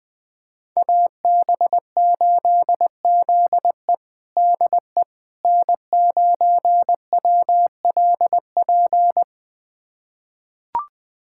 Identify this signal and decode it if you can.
Morse code, heard as a keyed tone, about 20 wpm, 700 Hz. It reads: AB8ZE DE N9WLP
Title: USA Callsign DE Callsign - 5 Characters - 20wpm